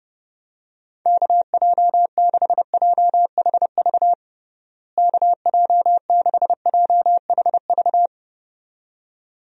Morse code, heard as a keyed tone, about 31 wpm, 700 Hz. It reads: KJ6JHV KJ6JHV
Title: USA Call Signs - 6 Characters - Repeated 2x - 30wpm